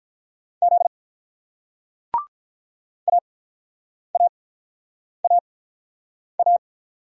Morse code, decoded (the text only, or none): Z AAAA